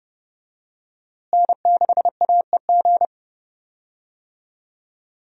Morse code, heard as a keyed tone, about 30 wpm, 700 Hz. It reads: N6AEG
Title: USA Call Signs - 5 Characters - Repeated 2x - 30wpm